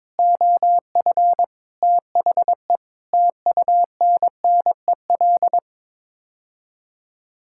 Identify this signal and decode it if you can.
Morse code, heard as a keyed tone, about 22 wpm, 700 Hz. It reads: OF THE TUNNEL
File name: practice / 22wpm / Sets of 8 Words - 22wpm